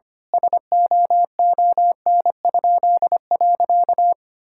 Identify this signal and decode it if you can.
Morse code, heard as a keyed tone, about 25 wpm, 700 Hz. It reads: SOON?.
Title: Pangram Challenge v2 - 25wpm